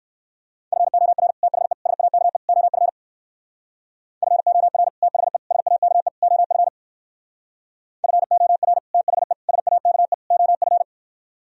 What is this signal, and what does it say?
Morse code, heard as a keyed tone, about 60 wpm, 700 Hz. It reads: FOR THE SAKE OF FOR THE SAKE OF FOR THE SAKE OF